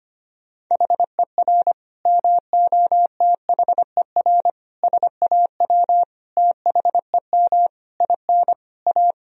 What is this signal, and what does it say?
Morse code, heard as a keyed tone, about 25 wpm, 700 Hz. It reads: HER MOTHER SAW THEM IN A
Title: Sentences from Top 1000 Words - Review Part 3 - 25wpm